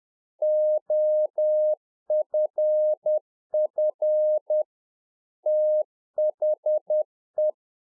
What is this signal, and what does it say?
Morse code, heard as a keyed tone, about 10 wpm, 610 Hz. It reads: OFF THE